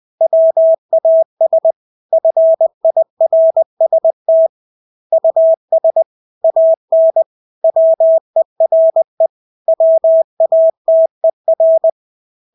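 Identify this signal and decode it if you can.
Morse code, heard as a keyed tone, about 20 wpm, 650 Hz. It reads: WAS FIRST US AN WERE WATER
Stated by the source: Morse practice of English words at 20 wpm.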